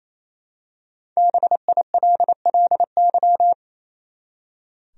Morse code, heard as a keyed tone, about 28 wpm, 700 Hz. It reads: BILLY
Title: USA Common Names - 28wpm